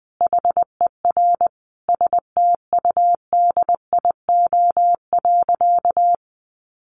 Morse code, heard as a keyed tone, about 20 wpm, 700 Hz. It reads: HER STUDIO.
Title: Sentences from Top 1200 Words 20wpm